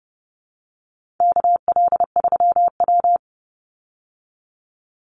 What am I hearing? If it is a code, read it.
Morse code, heard as a keyed tone, about 30 wpm, 700 Hz. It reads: KL3W
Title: Call Signs Repeated 2x - 30wpm